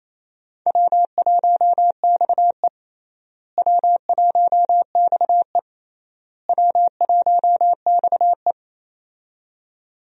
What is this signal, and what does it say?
Morse code, heard as a keyed tone, about 28 wpm, 700 Hz. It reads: W1XE W1XE W1XE
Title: USA Call Signs - 4 Characters - Repeated 3x - 28wpm